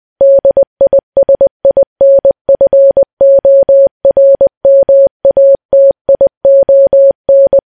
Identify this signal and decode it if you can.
Morse code, heard as a keyed tone, about 20 wpm, 550 Hz. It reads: DISINFORMATION
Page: news-headlines-in-morse-code-at-20-wpm